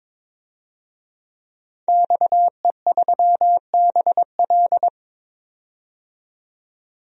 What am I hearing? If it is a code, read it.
Morse code, heard as a keyed tone, about 22 wpm, 700 Hz. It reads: XE3BL
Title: DXCC-100 Call Signs - 5 Characters - 22wpm